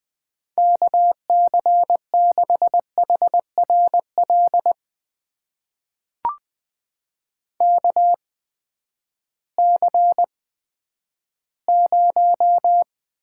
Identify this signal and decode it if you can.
Morse code, heard as a keyed tone, about 20 wpm, spaced as for 11 wpm, 700 Hz. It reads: KC6HRL KC0
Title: USA Call Signs - 6 Characters - 8x Character Spacing - 20wpm